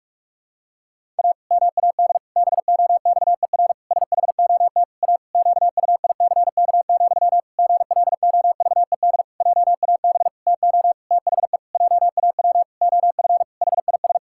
Morse code, heard as a keyed tone, about 46 wpm, 700 Hz. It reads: A MAD BOXER SHOT A QUICK, GLOVED JAB TO THE JAW OF HIS
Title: Pangram Challenge v2 - 45wpm